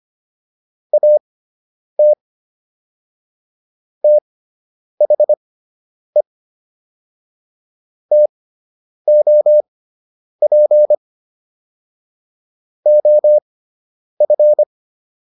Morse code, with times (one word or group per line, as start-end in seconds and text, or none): 0.9-2.1 AT
4.0-6.2 THE
8.1-10.9 TOP
12.9-14.6 OF